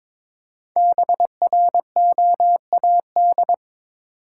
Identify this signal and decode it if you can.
Morse code, heard as a keyed tone, about 22 wpm, 690 Hz. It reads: BROAD
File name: 5 Letter Words - First 3 Letters Given - 22wpm